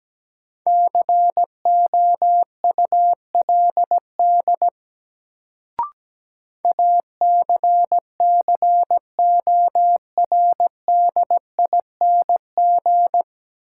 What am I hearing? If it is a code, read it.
Morse code, heard as a keyed tone, about 17 wpm, 700 Hz. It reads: COULD ACCORDING
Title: Sets of 2 Words - New Word Drill - 17wpm